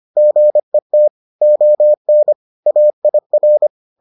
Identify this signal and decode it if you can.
Morse code, heard as a keyed tone, about 25 wpm, 600 Hz. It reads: GET ON AIR